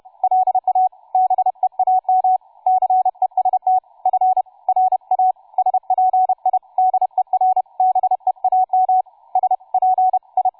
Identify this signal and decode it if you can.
Morse code, heard as a keyed tone, about 31 wpm, 750 Hz, with radio noise. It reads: LA BEAM CEST FRA SPIDER BEAM SPI
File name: cw forum / tu2 f5ldy 18mhz 16h46